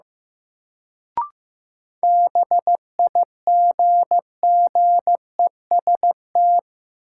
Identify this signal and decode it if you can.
Morse code, heard as a keyed tone, about 15 wpm, 700 Hz. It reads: BIGGEST
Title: Sets of 4 Words - New Word Drill - 15wpm